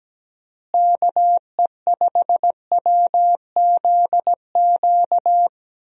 Morse code, heard as a keyed tone, about 17 wpm, 700 Hz. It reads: KE5WZQ
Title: USA Call Signs - 6 Characters - Repeated 2x - 17wpm